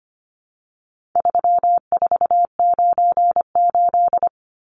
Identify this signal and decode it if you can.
Morse code, heard as a keyed tone, about 25 wpm, 700 Hz. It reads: 3498
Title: Numbers - 4-Digit 25wpm